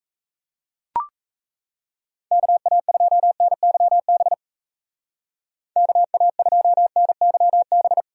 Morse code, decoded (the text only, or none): KA2NYB KA2NYB